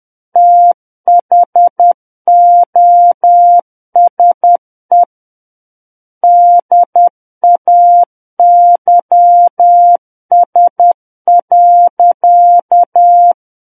Morse code, heard as a keyed tone, about 10 wpm, 700 Hz. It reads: THOSE DAYS.
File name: Book One / 01-Chapter 1 - 10 WPM